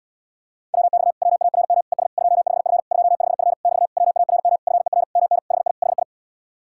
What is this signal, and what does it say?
Morse code, heard as a keyed tone, about 67 wpm, 700 Hz. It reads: WX WARM ES 25F 25F <BT> WARM FER MN HI HI